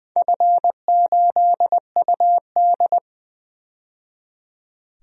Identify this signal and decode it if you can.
Morse code, heard as a keyed tone, about 20 wpm, 700 Hz. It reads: F8UD
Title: DXCC-100 Call Signs - 4 Characters - 20wpm